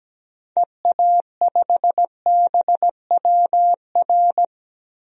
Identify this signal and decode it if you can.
Morse code, heard as a keyed tone, about 17 wpm, 700 Hz. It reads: EA5BWR
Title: DXCC-100 Call Signs - 6 Characters - Repeated 3x - 17wpm